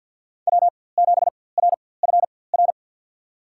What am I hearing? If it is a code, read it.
Morse code, heard as a keyed tone, about 49 wpm, spaced as for 23 wpm, 700 Hz. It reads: W7RFR
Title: USA Call Signs - 5 Characters - 4x Character Spacing - 50wpm